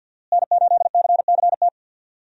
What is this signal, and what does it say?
Morse code, heard as a keyed tone, about 49 wpm, 700 Hz. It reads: N8CCT
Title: USA Call Signs - 5 Characters - Repeated 3x - 50wpm